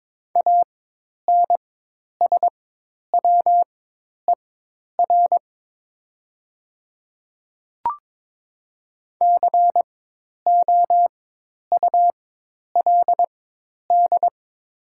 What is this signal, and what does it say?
Morse code, heard as a keyed tone, about 22 wpm, spaced as for 10 wpm, 700 Hz. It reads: ANSWER COULD
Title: 4x Character Spacing - Top 500 Words - 22wpm